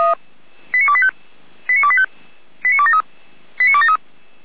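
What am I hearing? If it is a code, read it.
Five-tone selective call (ZVEI), heard as quick runs of five tones, tones of about 70 ms. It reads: 89285 89285 89273 89273
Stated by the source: ZVEI five-tone selective call received on an SDR.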